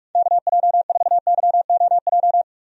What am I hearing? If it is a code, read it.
Morse code, heard as a keyed tone, about 44 wpm, 700 Hz. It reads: KJ4YOJ
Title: USA Call Signs - 6 Characters - 45wpm